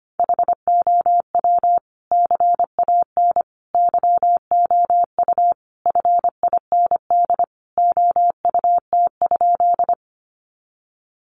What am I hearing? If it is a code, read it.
Morse code, heard as a keyed tone, about 25 wpm, 700 Hz. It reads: HOW CAN YOU FIND OUT?
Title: Sentences from Top 200 Words 25wpm